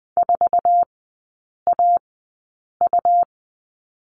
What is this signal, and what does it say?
Morse code, heard as a keyed tone, about 20 wpm, spaced as for 8 wpm, 700 Hz. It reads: 4AU
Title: 205 Lesson 33e - Letter j - Three Characters 20wpm